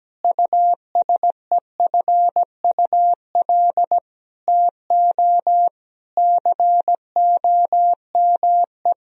Morse code, heard as a keyed tone, about 17 wpm, 700 Hz. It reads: USEFUL TO COME
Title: Sentences from Top 1000 Words - Review Part 4 - 17wpm